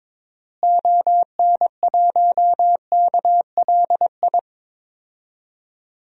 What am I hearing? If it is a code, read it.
Morse code, heard as a keyed tone, about 21 wpm, 700 Hz. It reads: ON1KLI
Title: DXCC-100 Call Signs - 6 Characters - Repeated 3x - 22wpm